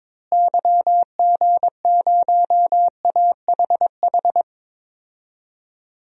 Morse code, heard as a keyed tone, about 21 wpm, 700 Hz. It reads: YG0AHH
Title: DXCC-100 Call Signs - 6 Characters - Repeated 3x - 22wpm